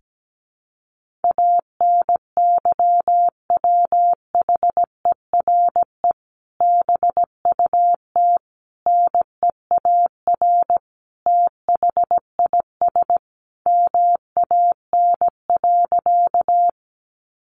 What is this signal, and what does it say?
Morse code, heard as a keyed tone, about 17 wpm, 700 Hz. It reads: ANYWHERE BUT NEAR THIS MAN.
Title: Sentences from Top 1400 Words 17wpm